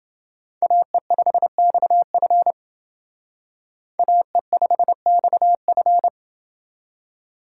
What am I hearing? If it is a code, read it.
Morse code, heard as a keyed tone, about 29 wpm, 700 Hz. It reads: AE5XF AE5XF